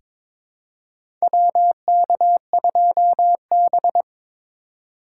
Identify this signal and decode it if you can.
Morse code, heard as a keyed tone, about 21 wpm, 700 Hz. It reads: WK2B
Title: USA Call Signs - 4 Characters - 3x Character Spacing - 22wpm